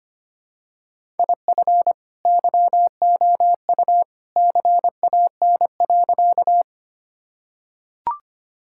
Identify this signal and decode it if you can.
Morse code, heard as a keyed tone, about 25 wpm, 700 Hz. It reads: IF YOU CAN.
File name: practice / 25wpm / Sentences with 2-Word Phrases - 25wpm